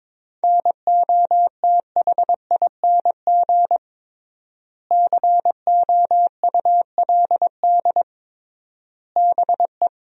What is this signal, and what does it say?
Morse code, heard as a keyed tone, about 22 wpm, 700 Hz. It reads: NOTHING COULD BE
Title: 3x Word-Spacing - Sentences from Top 500 Words - 22wpm